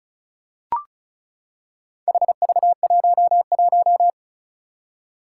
Morse code, heard as a keyed tone, about 36 wpm, 700 Hz. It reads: HV11
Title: Four Challenging Letter-Number Combinations 35wpm